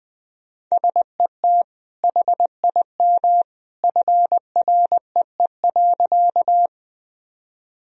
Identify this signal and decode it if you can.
Morse code, heard as a keyed tone, about 20 wpm, 700 Hz. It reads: SET HIM FREE.
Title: Sentences from Top 1000 Words - Review Part 1 - 20wpm